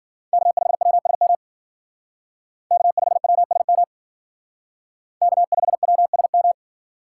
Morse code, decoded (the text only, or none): K5WSM K5WSM K5WSM